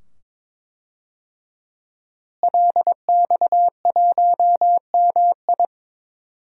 Morse code, heard as a keyed tone, about 22 wpm, 700 Hz. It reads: LX1MI
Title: DXCC-100 Call Signs - 5 Characters - Repeated 2x - 22wpm